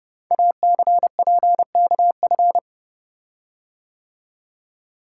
Morse code, heard as a keyed tone, about 30 wpm, 700 Hz. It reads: ACPKF
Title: Code Groups - 5 Letter:Numbers - 30wpm